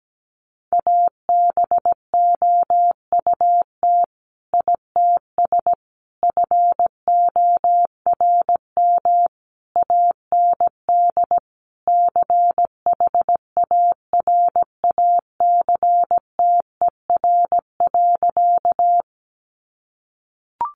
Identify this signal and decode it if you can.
Morse code, heard as a keyed tone, about 17 wpm, 700 Hz. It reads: ABOUT ITS FORM AND CHARACTER.
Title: Sentences from Top 1200 Words 17wpm